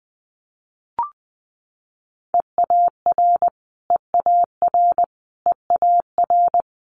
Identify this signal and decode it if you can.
Morse code, heard as a keyed tone, about 19 wpm, 700 Hz. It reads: EAR EAR EAR
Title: Multi-letter Phonemes Repeated 3x 20wpm